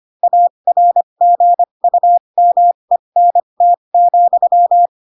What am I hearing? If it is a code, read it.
Morse code, heard as a keyed tone, about 25 wpm, spaced as for 22 wpm, 700 Hz. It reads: ARGUMENT,
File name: Pinocchio / Pinocchio ch3 18 25 ik1hge